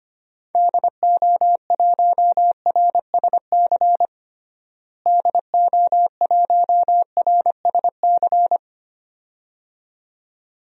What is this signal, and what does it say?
Morse code, heard as a keyed tone, about 25 wpm, 700 Hz. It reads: DO1RSC DO1RSC
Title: DXCC-100 Call Signs - 6 Characters - Repeated 2x - 25wpm